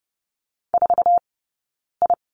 Morse code, heard as a keyed tone, about 30 wpm, spaced as for 9 wpm, 690 Hz. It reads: 4I
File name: Lesson 19d - Letter w - Two Characters 30wpm